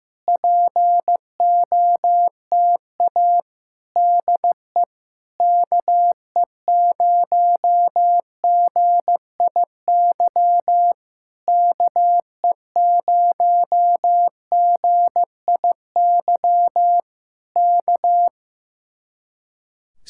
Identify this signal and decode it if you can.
Morse code, heard as a keyed tone, about 15 wpm, 700 Hz. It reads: POTA DE KE0GIY KE0GIY K
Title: Speed-Racing Common QSO Phrases Encore - 15wpm